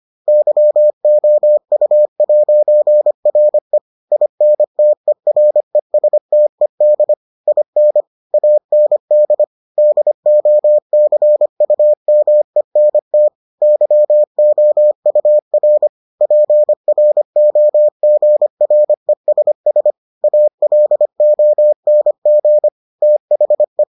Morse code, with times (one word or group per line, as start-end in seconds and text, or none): 0.3-3.8 YOU'RE
4.1-7.1 INTERESTED
7.5-8.0 IN
8.3-9.4 AND
9.8-13.3 DOCUMENT
13.6-15.9 YOUR
16.2-19.9 PROGRESS
20.2-22.7 ALONG
23.0-23.8 THE